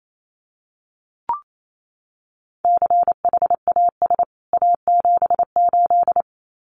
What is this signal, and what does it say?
Morse code, heard as a keyed tone, about 28 wpm, 700 Hz. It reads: CHAS A78
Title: CWT Exchange 28wpm